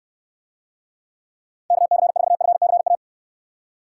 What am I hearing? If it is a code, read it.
Morse code, heard as a keyed tone, about 69 wpm, 700 Hz. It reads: KC4FCA